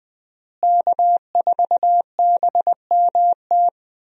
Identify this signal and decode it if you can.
Morse code, heard as a keyed tone, about 20 wpm, 700 Hz. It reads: K4BMT
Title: USA Call Signs - 5 Characters - Repeated 3x - 20wpm